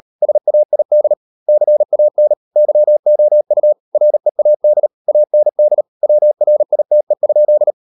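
Morse code, heard as a keyed tone, about 38 wpm, 600 Hz. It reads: SAID CAN YOU READ AND WRITE?
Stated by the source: Morse code practice at 38 WPM, sending Ken Follett's novel Never in English, from CW med Gnister.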